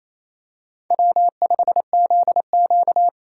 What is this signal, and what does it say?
Morse code, heard as a keyed tone, about 28 wpm, 700 Hz. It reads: W5ZQ
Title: USA Call Signs - 4 Characters - 2x Character Spacing - 28wpm